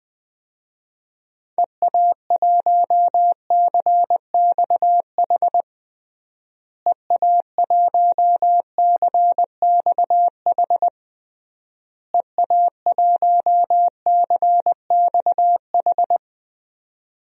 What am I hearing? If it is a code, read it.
Morse code, heard as a keyed tone, about 20 wpm, 700 Hz. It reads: EA1CXH EA1CXH EA1CXH
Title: DXCC-100 Call Signs - 6 Characters - Repeated 3x - 20wpm